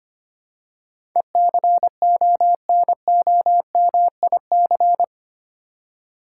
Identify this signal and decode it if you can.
Morse code, heard as a keyed tone, about 25 wpm, 700 Hz. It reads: ECONOMIC
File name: 8 Letter Words - First 3 Letters Given - 25wpm